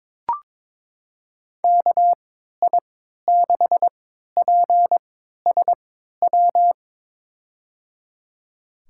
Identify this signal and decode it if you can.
Morse code, heard as a keyed tone, about 22 wpm, spaced as for 12 wpm, 700 Hz. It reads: KI6PSW
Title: USA Call Signs - 3x Character Spacing - 22wpm